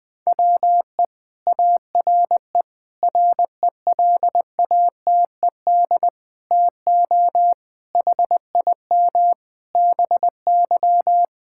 Morse code, heard as a keyed tone, about 21 wpm, 700 Hz. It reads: WE ARE RELATED TO HIM BY
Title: Sentences from Top 1000 Words - Review Part 5 - 20wpm